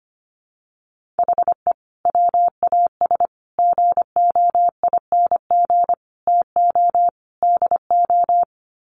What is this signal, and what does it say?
Morse code, heard as a keyed tone, about 25 wpm, 700 Hz. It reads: HE WAS GOING TO DO
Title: Sets of 5 Words Encore2 25wpm